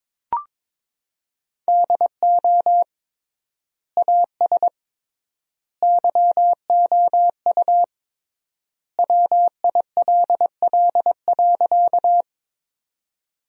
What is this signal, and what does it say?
Morse code, heard as a keyed tone, about 22 wpm, 700 Hz. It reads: DO AS YOU WILL.
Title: 3x Word-Spacing - Sentences from Top 100 Words - 22wpm